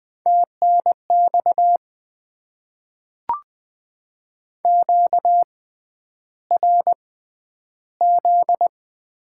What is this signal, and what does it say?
Morse code, heard as a keyed tone, about 20 wpm, spaced as for 10 wpm, 700 Hz. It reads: TNX QRZ
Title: Top 90 QSO Elements - 6x Character Spacing - 20wpm